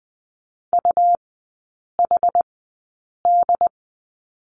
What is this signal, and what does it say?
Morse code, heard as a keyed tone, about 20 wpm, spaced as for 8 wpm, 700 Hz. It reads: UHD
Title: 138 Lesson 23e - Letter f - Three Characters 20wpm